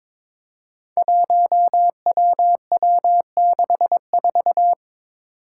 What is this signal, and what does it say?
Morse code, heard as a keyed tone, about 22 wpm, 700 Hz. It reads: 1WW64
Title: Five Challenging Letter-Number Combinations 22wpm